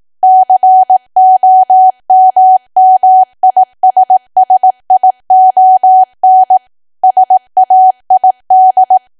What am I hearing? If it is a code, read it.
Morse code, heard as a keyed tone, about 18 wpm, 750 Hz. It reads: COMMISSION SAID
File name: cwa / W1AW-Level2 / 18WPM